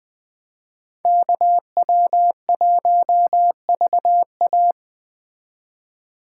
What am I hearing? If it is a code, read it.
Morse code, heard as a keyed tone, about 20 wpm, 700 Hz. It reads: KW1VA
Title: Speed Racing - USA 2x2 Call Signs 17wpm